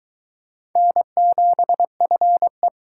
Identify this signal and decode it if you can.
Morse code, heard as a keyed tone, about 22 wpm, 700 Hz. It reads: N7FE